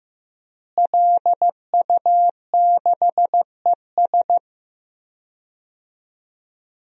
Morse code, heard as a keyed tone, about 15 wpm, 700 Hz. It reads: LU6ES
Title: DXCC-100 Call Signs - 5 Characters - Repeated 2x - 15wpm